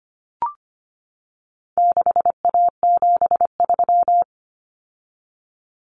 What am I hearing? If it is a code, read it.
Morse code, heard as a keyed tone, about 25 wpm, 700 Hz. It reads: 6A73
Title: Common Cut Numbers - 4-Digit 25wpm